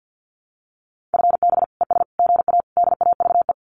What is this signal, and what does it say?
Morse code, heard as a keyed tone, about 50 wpm, 700 Hz. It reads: FB ES GA DAVE
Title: Common QSO Phrases Encore 50wpm